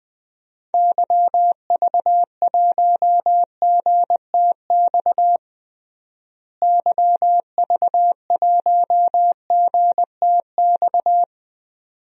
Morse code, heard as a keyed tone, about 21 wpm, 700 Hz. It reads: YV1GTX YV1GTX
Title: DXCC-100 Call Signs - 6 Characters - Repeated 2x - 20wpm